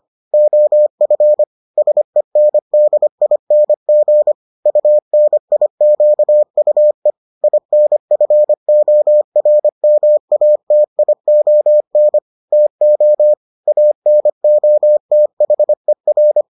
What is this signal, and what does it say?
Morse code, heard as a keyed tone, about 25 wpm, 600 Hz. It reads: OF SENDING UNIQUE INFORMATION TO ANOTHER